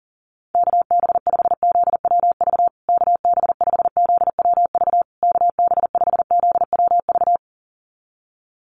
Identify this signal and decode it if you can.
Morse code, heard as a keyed tone, about 40 wpm, 700 Hz. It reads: KB5ZWV KB5ZWV KB5ZWV
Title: USA Call Signs Repeated 3x 40wpm